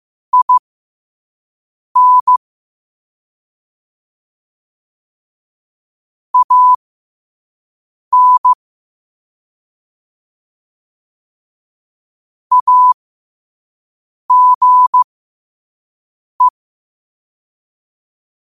Morse code, heard as a keyed tone, about 15 wpm, spaced as for 5 wpm, 1000 Hz. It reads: IN AN AGE
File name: qotdfiles / 5wpm